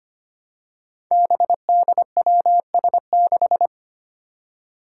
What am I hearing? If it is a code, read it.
Morse code, heard as a keyed tone, about 25 wpm, 700 Hz. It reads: BDWS6
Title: Five Challenging Letter-Number Combinations 25wpm